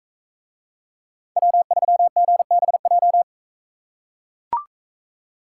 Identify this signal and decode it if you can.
Morse code, heard as a keyed tone, about 41 wpm, 700 Hz. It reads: W3GBJ